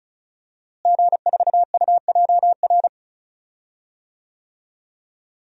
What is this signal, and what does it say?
Morse code, heard as a keyed tone, about 36 wpm, 700 Hz. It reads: G4UJR
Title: DXCC-100 Call Signs - 5 Characters - Repeated 2x - 35wpm